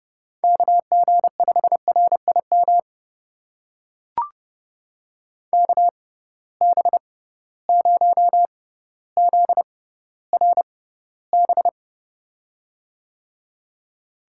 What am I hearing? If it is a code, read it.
Morse code, heard as a keyed tone, about 30 wpm, spaced as for 15 wpm, 700 Hz. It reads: KG5RIM KB0ZRB
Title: USA Call Signs - 6 Characters - 6x Character Spacing - 30wpm